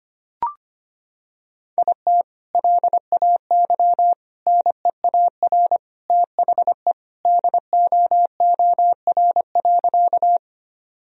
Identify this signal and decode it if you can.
Morse code, heard as a keyed tone, about 25 wpm, 700 Hz. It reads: IT LAY NEAR THE DOOR.
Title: Sentences from Top 1000 Words - Review Part 2 - 25wpm